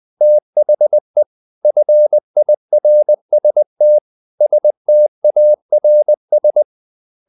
Morse code, heard as a keyed tone, about 20 wpm, 600 Hz. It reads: THE FIRST STARS
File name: the old man and the sea part 2 20wpm